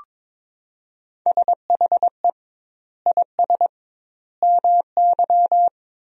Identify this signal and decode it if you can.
Morse code, heard as a keyed tone, about 22 wpm, 700 Hz. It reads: SHE IS MY
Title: 2x Word-Spacing - Sentences from Top 300 Words - 22wpm